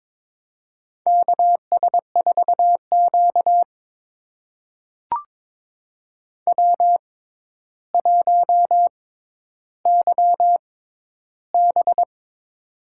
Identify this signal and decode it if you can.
Morse code, heard as a keyed tone, about 22 wpm, spaced as for 11 wpm, 700 Hz. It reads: KS4Q W1YB
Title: USA Call Signs - 4 Characters - 6x Character Spacing - 22wpm